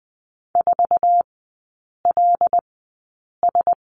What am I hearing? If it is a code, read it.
Morse code, heard as a keyed tone, about 20 wpm, spaced as for 8 wpm, 700 Hz. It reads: 4LS
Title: 074 Lesson 13e - Letter L - Three Characters 20wpm